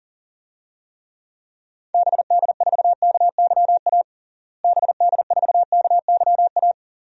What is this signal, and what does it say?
Morse code, heard as a keyed tone, about 40 wpm, 700 Hz. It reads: BD4KYA BD4KYA
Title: DXCC-100 Call Signs - 6 Characters - Repeated 2x - 40wpm